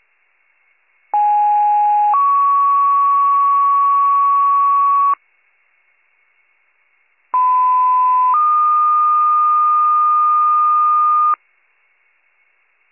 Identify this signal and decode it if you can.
Two-tone page, heard as a one-second tone, then a three-second tone, about 830 then 1120 Hz, about 1010 then 1220 Hz. It is heard from 1.1 to 11.3 s.